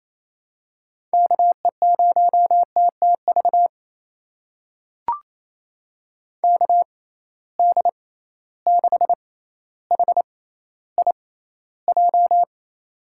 Morse code, heard as a keyed tone, about 27 wpm, spaced as for 14 wpm, 700 Hz. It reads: KE0TTV KD6HIJ
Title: USA Call Signs - 6 Characters - 6x Character Spacing - 28wpm